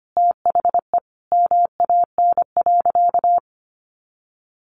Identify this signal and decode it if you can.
Morse code, heard as a keyed tone, about 25 wpm, 700 Hz. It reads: THE MAN.
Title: Sentences from Top 400 Words 25wpm